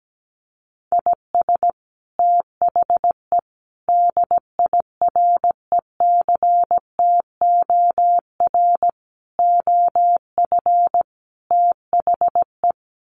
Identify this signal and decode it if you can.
Morse code, heard as a keyed tone, about 17 wpm, 700 Hz. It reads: IS THE DIRECTOR OF THE
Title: Sets of 5 Words Encore2 17wpm